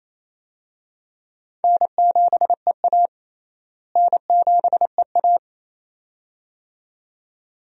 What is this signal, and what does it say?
Morse code, heard as a keyed tone, about 27 wpm, 700 Hz. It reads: N7EA N7EA
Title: USA Call Signs - 4 Characters - Repeated 2x - 28wpm